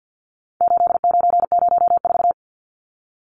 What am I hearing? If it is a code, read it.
Morse code, heard as a keyed tone, about 50 wpm, 700 Hz. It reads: 8904
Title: Numbers - 4-Digit 50wpm